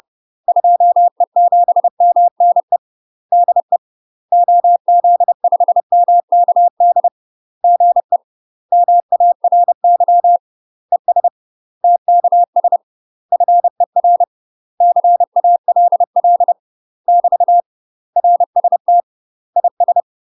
Morse code, heard as a keyed tone, about 30 wpm, 700 Hz. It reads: 2E7MNE DE OZ5MKD GE MARY ES TKS FER CALL <BT> RST IS
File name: UK MASTER QSOs TRAINING / 30WPM / UK MASTER QSOs 0036